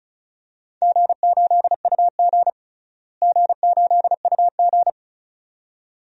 Morse code, heard as a keyed tone, about 34 wpm, 700 Hz. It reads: G8UG G8UG
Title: DXCC-100 Call Signs - 4 Characters - Repeated 2x - 35wpm